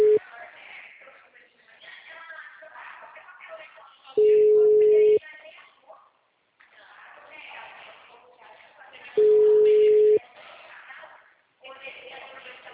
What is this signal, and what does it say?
Ringback tone (Europe, 425 Hz), with speech.